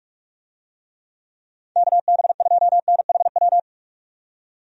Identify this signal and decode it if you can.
Morse code, heard as a keyed tone, about 46 wpm, 700 Hz. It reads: KB2NHW
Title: USA Call Signs - 6 Characters - 45wpm